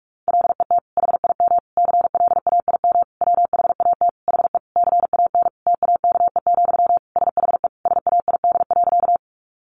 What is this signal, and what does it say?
Morse code, heard as a keyed tone, about 44 wpm, 700 Hz. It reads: LET HIM CLAIM WHAT HE CAN TAKE, SHE SAID.
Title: Sentences from Top 1500 Words 45wpm